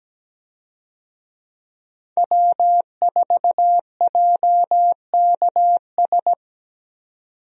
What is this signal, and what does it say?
Morse code, heard as a keyed tone, about 17 wpm, 700 Hz. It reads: W4JKS
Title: USA Call Signs - 5 Characters - 17wpm